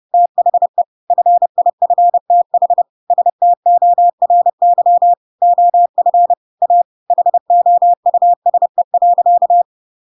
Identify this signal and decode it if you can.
Morse code, heard as a keyed tone, about 30 wpm, 700 Hz. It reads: THE FIFTH STORY OF A HOUSE.
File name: Pinocchio / Pinocchio ch7 30 30 ik1hge